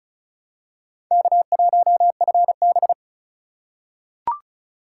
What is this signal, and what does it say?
Morse code, heard as a keyed tone, about 34 wpm, 700 Hz. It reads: K1FB
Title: USA Call Signs - 4 Characters - 4x Character Spacing - 35wpm